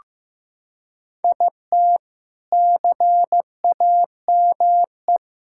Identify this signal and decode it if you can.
Morse code, heard as a keyed tone, about 15 wpm, 700 Hz. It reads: IT CAME
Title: Sentences from Top 1000 Words - Review Part 5 - 15wpm